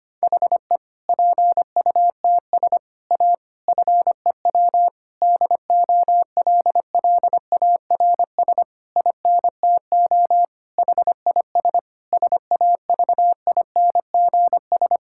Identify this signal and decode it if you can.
Morse code, heard as a keyed tone, about 25 wpm, 700 Hz. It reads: HE PUTS A FEW DOLLARS INTO HIS SAVINGS
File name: practice / 25wpm / Difficult Sentences with Hints - 25wpm